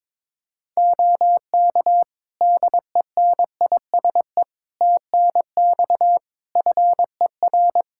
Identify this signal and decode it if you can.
Morse code, heard as a keyed tone, about 22 wpm, 700 Hz. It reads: OK DENISE TNX FER
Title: Practice QSO - Exchange 4 - Ending - 22wpm